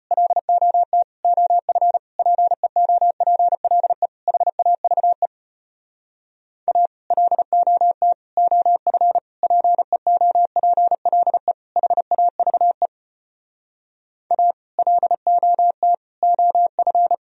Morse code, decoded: LOT OF PEOPLE HAVE A LOT OF PEOPLE HAVE A LOT OF

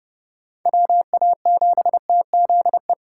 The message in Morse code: WA7TZE

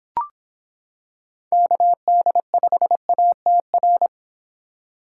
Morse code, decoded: KD5ATR

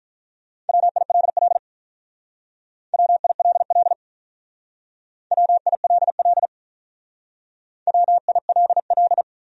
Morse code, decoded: WILL WILL WILL WILL

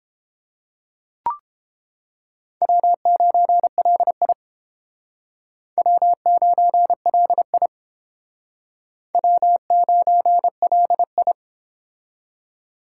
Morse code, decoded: W9LI W9LI W9LI

